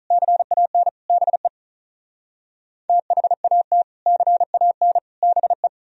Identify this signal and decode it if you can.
Morse code, heard as a keyed tone, about 44 wpm, 700 Hz. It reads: CAN BE THAT CAN BE